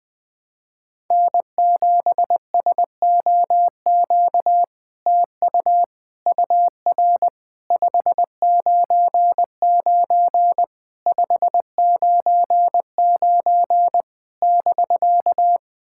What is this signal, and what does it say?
Morse code, heard as a keyed tone, about 21 wpm, 700 Hz. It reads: N7SOQ TU UR 599 599 <BK>
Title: SOTA QSO Practice 20wpm